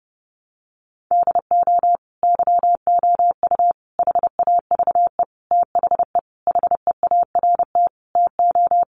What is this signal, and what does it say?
Morse code, heard as a keyed tone, about 30 wpm, 700 Hz. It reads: DO YOU HAVE THE HEART TO
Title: Sentences from Top 1200 Words 30wpm